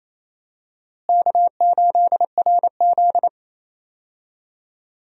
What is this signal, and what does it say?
Morse code, heard as a keyed tone, about 27 wpm, 700 Hz. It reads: K8RZ
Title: USA Call Signs - 4 Characters - Repeated 2x - 28wpm